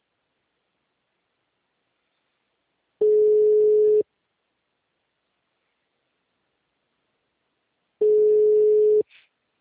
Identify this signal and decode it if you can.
Ringback tone (Europe, 425 Hz).